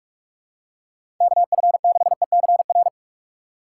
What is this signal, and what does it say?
Morse code, heard as a keyed tone, about 46 wpm, 700 Hz. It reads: KF6ECR
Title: USA Call Signs - 6 Characters - 6x Character Spacing - 45wpm